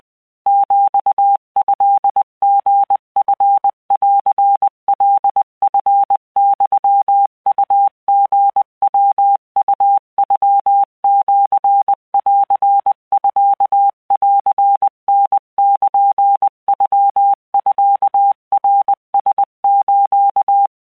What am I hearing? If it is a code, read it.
Morse code (Japanese Wabun code), heard as a keyed tone, about 20 wpm, 800 Hz. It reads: ヒトリチンカチユウリヤウノシンミンタルノミナラス